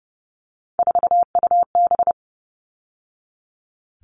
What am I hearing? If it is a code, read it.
Morse code, heard as a keyed tone, about 30 wpm, 700 Hz. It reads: 4UB
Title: All Cut Numbers - 3-Digit 30wpm